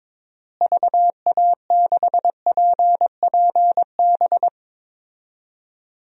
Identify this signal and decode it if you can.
Morse code, heard as a keyed tone, about 22 wpm, 700 Hz. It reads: VA6PPB
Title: DXCC-100 Call Signs - 6 Characters - Repeated 2x - 22wpm